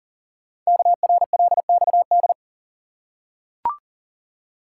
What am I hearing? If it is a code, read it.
Morse code, heard as a keyed tone, about 42 wpm, 690 Hz. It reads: KRLXD
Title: Code Groups - 5 Letters - 40wpm